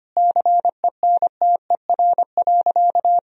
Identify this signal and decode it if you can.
Morse code, heard as a keyed tone, about 25 wpm, 700 Hz. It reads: CENTER.
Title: Sentences from Top 1000 Words - Review Part 2 - 25wpm